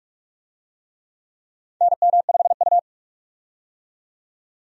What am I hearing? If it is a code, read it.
Morse code, heard as a keyed tone, about 46 wpm, 700 Hz. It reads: NM5U
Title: USA Call Signs - 4 Characters - 45wpm